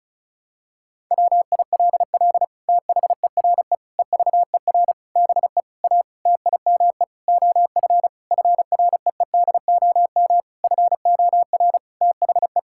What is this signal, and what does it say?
Morse code, heard as a keyed tone, about 36 wpm, 700 Hz. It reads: WILL THERE EVER BE A TIME OF FREEDOM FOR THE
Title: Sentences from Top 1000 Words - Review Part 5 - 35wpm